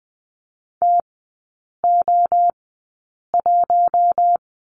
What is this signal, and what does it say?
Morse code, heard as a keyed tone, about 20 wpm, spaced as for 8 wpm, 700 Hz. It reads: TO1
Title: 093 Lesson 16e - Letter c - Three Characters 20wpm